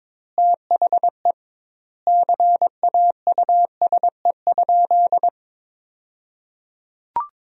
Morse code, heard as a keyed tone, about 22 wpm, 700 Hz. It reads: THE CAUSE?
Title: 2x Word-Spacing - Sentences from Top 200 Words - 22wpm